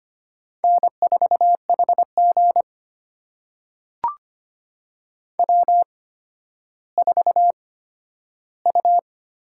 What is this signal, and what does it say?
Morse code, heard as a keyed tone, about 25 wpm, spaced as for 12 wpm, 700 Hz. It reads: N4HG W4U
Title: USA Call Signs - 4 Characters - 8x Character Spacing - 25wpm